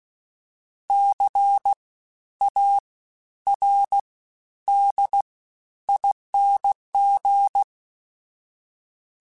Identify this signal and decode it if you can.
Morse code, heard as a keyed tone, about 16 wpm, 800 Hz. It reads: C A R D ING
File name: Special INGs-mp3s / Special INGs-mp3s / SlowToFaster ing @15